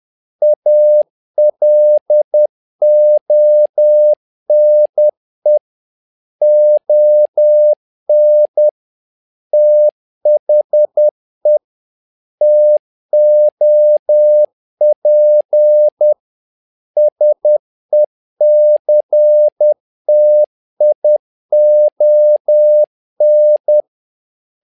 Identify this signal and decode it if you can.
Morse code, heard as a keyed tone, about 10 wpm, 600 Hz. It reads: ALONE ON THE TOP SECTION
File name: learning center / tower trouble / tower trouble 10wpm